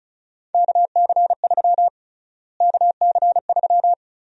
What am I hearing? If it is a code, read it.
Morse code, heard as a keyed tone, about 36 wpm, 700 Hz. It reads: KC3 KC3
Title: USA Call Sign Prefixes - Repeated 2x - 35wpm